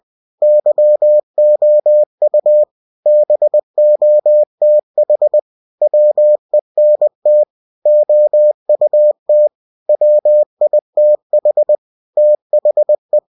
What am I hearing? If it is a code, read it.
Morse code, heard as a keyed tone, about 20 wpm, 600 Hz. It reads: YOU BOTH WENT OUT WITH THE
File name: cw-books / winnie-the-pooh / winnie-20wpm-page-0006